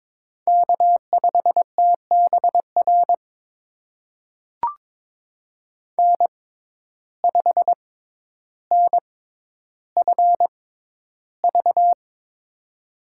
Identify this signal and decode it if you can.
Morse code, heard as a keyed tone, about 23 wpm, spaced as for 11 wpm, 700 Hz. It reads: K5TBR N5NFV
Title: USA Call Signs - 5 Characters - 6x Character Spacing - 22wpm